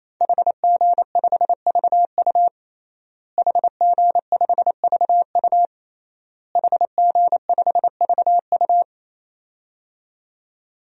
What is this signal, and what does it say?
Morse code, heard as a keyed tone, about 28 wpm, 700 Hz. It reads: HG5VU HG5VU HG5VU
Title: DXCC-100 Call Signs - 5 Characters - Repeated 3x - 28wpm